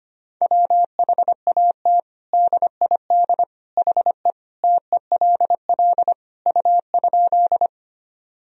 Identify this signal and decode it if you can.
Morse code, heard as a keyed tone, about 25 wpm, 700 Hz. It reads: WHAT DID HE TELL U?
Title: Sentences from Top 200 Words and Abbreviations 25wpm